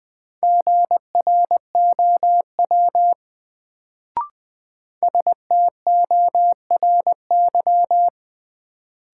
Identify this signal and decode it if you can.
Morse code, heard as a keyed tone, about 20 wpm, 700 Hz. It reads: GROW STORY
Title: Sentences from Top 300 Words - New Word Drill - 20wpm